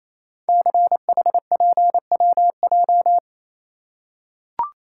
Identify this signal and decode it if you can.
Morse code, heard as a keyed tone, about 28 wpm, 700 Hz. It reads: CHPWJ
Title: Code Groups - 5 Letters - 28wpm